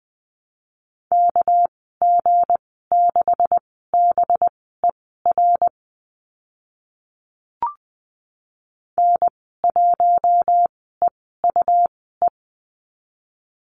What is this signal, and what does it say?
Morse code, heard as a keyed tone, about 20 wpm, spaced as for 14 wpm, 700 Hz. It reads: KG6BER N1EUE
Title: 238 Lesson 38g - Letter q - Call Signs 20wpm